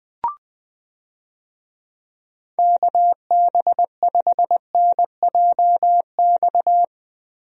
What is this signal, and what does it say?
Morse code, heard as a keyed tone, about 20 wpm, 700 Hz. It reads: KB5NJX
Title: USA Call Signs - 6 Characters - Repeated 2x - 20wpm